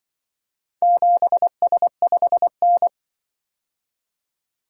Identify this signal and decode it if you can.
Morse code, heard as a keyed tone, about 24 wpm, 700 Hz. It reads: 7S5N
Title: Speed-Racing - Four Letter-Number Combinations - 20wpm